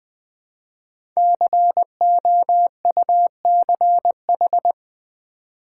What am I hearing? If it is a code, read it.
Morse code, heard as a keyed tone, about 20 wpm, 700 Hz. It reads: COUCH
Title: Easy Words with Hints - 20wpm